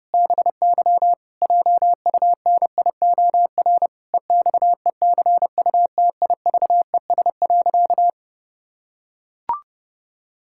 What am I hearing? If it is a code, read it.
Morse code, heard as a keyed tone, about 30 wpm, 700 Hz. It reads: BY JUNIOR EXECUTIVES.
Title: Pangram Challenge v2 - 30wpm